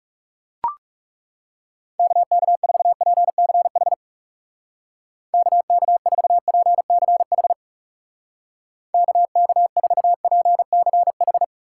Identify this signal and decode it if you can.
Morse code, heard as a keyed tone, about 38 wpm, 700 Hz. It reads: KK4PCH KK4PCH KK4PCH